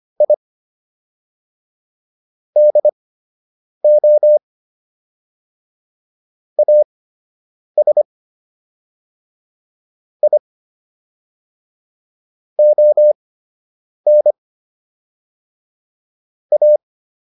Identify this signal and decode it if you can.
Morse code, heard as a keyed tone, about 25 wpm, spaced as for 8 wpm, 600 Hz. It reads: I DO AS I ON A